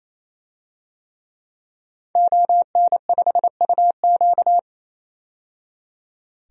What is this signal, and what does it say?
Morse code, heard as a keyed tone, about 28 wpm, 700 Hz. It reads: ON5UQ
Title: DXCC-100 Call Signs - 5 Characters - 28wpm